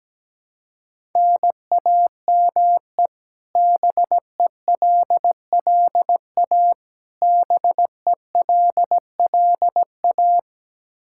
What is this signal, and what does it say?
Morse code, heard as a keyed tone, about 17 wpm, 700 Hz. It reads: NAME BELLA BELLA
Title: Practice QSO - Exchange 1 - Beginning - 17wpm